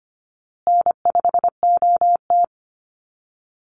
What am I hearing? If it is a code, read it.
Morse code, heard as a keyed tone, about 25 wpm, 700 Hz. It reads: N5OT
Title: Easy Call Signs Repeated 3x - 25wpm